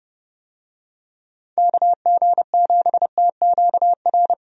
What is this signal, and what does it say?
Morse code, heard as a keyed tone, about 31 wpm, 700 Hz. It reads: KG7TQR
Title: USA Call Signs - 6 Characters - 30wpm